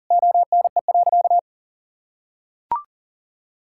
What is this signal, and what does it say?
Morse code, heard as a keyed tone, about 40 wpm, 700 Hz. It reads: ONE.
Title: Sentences from Top 1000 Words - Review Part 5 - 40wpm